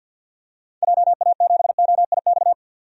Morse code, heard as a keyed tone, about 49 wpm, 700 Hz. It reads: JA7OIX
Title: DXCC-100 Call Signs - 6 Characters - 50wpm